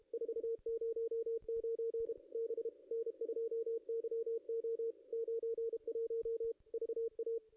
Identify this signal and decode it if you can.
Morse code, heard as a keyed tone, about 30 wpm, 440 Hz, with radio noise. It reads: 409 B N2YO 91 VA
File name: contests / ss-cw-k5tr-op-n5rz -0400